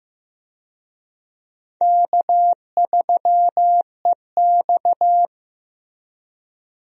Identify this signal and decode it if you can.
Morse code, heard as a keyed tone, about 15 wpm, 700 Hz. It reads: K3EX
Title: USA Call Signs - 4 Characters - Repeated 3x - 15wpm